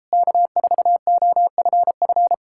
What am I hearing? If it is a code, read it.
Morse code, heard as a keyed tone, about 38 wpm, 700 Hz. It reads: K4OFF